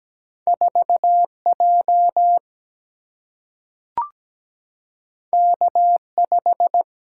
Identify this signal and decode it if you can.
Morse code, heard as a keyed tone, about 17 wpm, 700 Hz. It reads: 4 J K 5
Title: Single-Letter Single-Number 17wpm